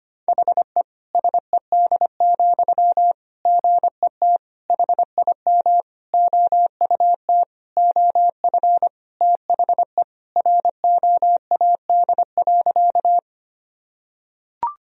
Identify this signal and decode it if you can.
Morse code, heard as a keyed tone, about 25 wpm, 700 Hz. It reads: HE SED, GET HIM OUT OF THE ROAD.
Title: Sentences from Top 400 Words and Abbreviations 25wpm